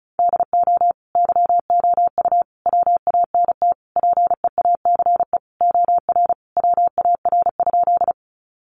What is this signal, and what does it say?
Morse code, heard as a keyed tone, about 34 wpm, 700 Hz. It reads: DO YOU WANT PEACE OR WAR?
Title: Sentences from Top 1500 Words 35wpm